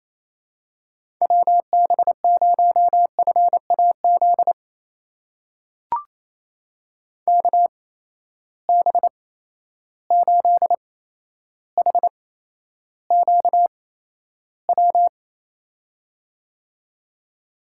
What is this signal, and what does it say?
Morse code, heard as a keyed tone, about 28 wpm, spaced as for 12 wpm, 700 Hz. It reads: WB0FAZ KB8HQW
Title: USA Call Signs - 6 Characters - 8x Character Spacing - 28wpm